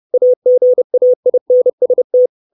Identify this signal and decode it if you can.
Morse code, heard as a keyed tone, about 30 wpm, 500 Hz. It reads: AGAINST
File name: 632 Words 30WPM